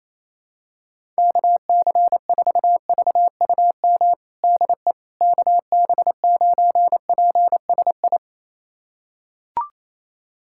Morse code, heard as a keyed tone, about 28 wpm, 700 Hz. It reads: KC4VUM DE KB9PSI
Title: USA Callsign DE Callsign - 6 Characters - 28wpm